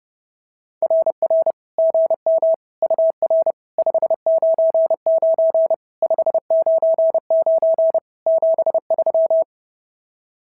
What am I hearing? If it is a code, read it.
Morse code, heard as a keyed tone, about 30 wpm, 590 Hz. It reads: RR GM UR 599 599 73
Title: SOTA QSO Practice 30wpm